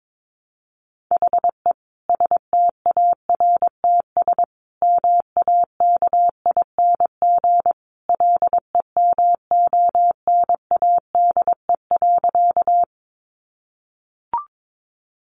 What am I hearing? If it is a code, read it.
Morse code, heard as a keyed tone, about 22 wpm, 700 Hz. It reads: HE STARTS MAKING LEMONADE.
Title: Additional Easy Phrases 22wpm